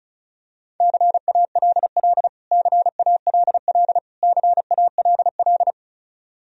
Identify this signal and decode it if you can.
Morse code, heard as a keyed tone, about 34 wpm, 700 Hz. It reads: CALL CALL CALL
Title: Top 100 Words Repeated 35wpm